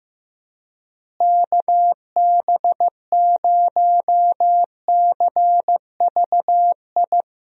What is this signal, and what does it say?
Morse code, heard as a keyed tone, about 15 wpm, 700 Hz. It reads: KB0CVI
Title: USA Call Signs - 6 Characters - Repeated 2x - 15wpm